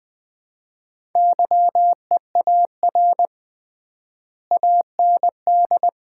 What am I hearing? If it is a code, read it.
Morse code, heard as a keyed tone, about 20 wpm, 700 Hz. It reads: YEAR AND
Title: 3x Word-Spacing - Sets of 4 Words - 20wpm